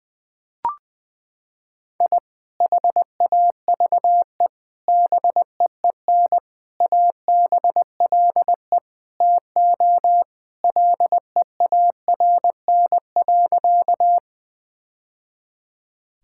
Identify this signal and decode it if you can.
Morse code, heard as a keyed tone, about 20 wpm, 700 Hz. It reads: I HAVE BEEN ABLE TO LEARN.
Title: Sentences with 3-Word Phrases - 20wpm